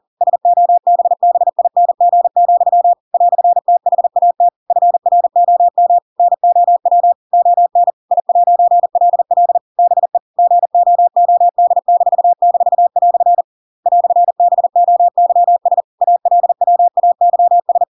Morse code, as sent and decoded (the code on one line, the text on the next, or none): ... --- -... -... .. -. --. --..-- / .-..-. - .... .- - / ..-. .-. --- -- / -. --- .-- / --- -. / .. .----. .-.. .-.. / -... / --. --- --- -.. -....- -....- .-..-. / .-..-. -... --- -.-- ... / .- .-.. .-- .- -.-- ...
SOBBING, "THAT FROM NOW ON I'LL BE GOOD--" "BOYS ALWAYS